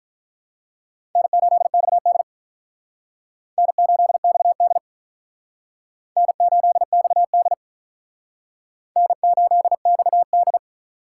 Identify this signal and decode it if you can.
Morse code, heard as a keyed tone, about 44 wpm, 700 Hz. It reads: N8XD N8XD N8XD N8XD